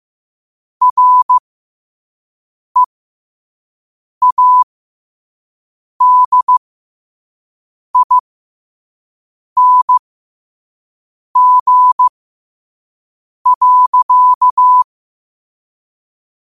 Morse code, heard as a keyed tone, about 15 wpm, spaced as for 5 wpm, 1000 Hz. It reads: READING.
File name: qotdfiles / 5wpm